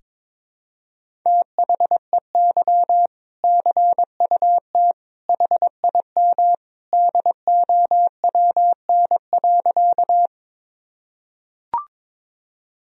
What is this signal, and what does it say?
Morse code, heard as a keyed tone, about 22 wpm, 700 Hz. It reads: THEY CUT HIM DOWN.
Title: Sentences from Top 1000 Words - Review Part 1 - 22wpm